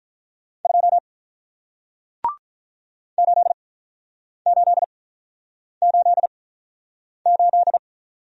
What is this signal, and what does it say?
Morse code, heard as a keyed tone, about 48 wpm, spaced as for 9 wpm, 700 Hz. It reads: 2 8888